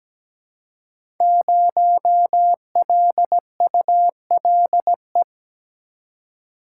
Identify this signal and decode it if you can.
Morse code, heard as a keyed tone, about 17 wpm, 700 Hz. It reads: 0LULE
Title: Code Groups - 5 Letter:Numbers - 17wpm